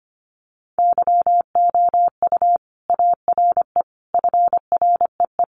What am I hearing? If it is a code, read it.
Morse code, heard as a keyed tone, about 25 wpm, 700 Hz. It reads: YOU ARE FREE
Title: Sentences from Top 500 Words 25wpm